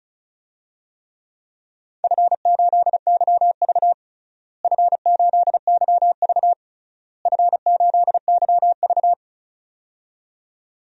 Morse code, heard as a keyed tone, about 36 wpm, 700 Hz. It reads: F8YV F8YV F8YV
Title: DXCC-100 Call Signs - 4 Characters - Repeated 3x - 35wpm